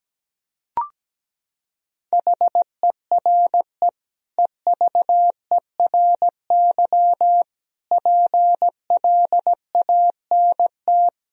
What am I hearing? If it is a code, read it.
Morse code, heard as a keyed tone, about 17 wpm, 700 Hz. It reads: HERE EVERY PLANT
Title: Sentences from Top 1000 Words - Review Part 1 - 17wpm